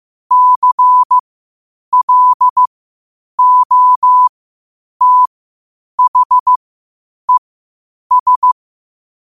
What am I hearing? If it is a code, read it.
Morse code, heard as a keyed tone, about 15 wpm, spaced as for 8 wpm, 1000 Hz. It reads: CLOTHES
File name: qotdfiles / 8wpm